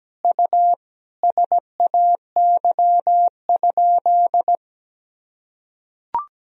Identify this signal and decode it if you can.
Morse code, heard as a keyed tone, about 17 wpm, 700 Hz. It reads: U SAY?
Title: Sentences from Top 200 Words and Abbreviations 17wpm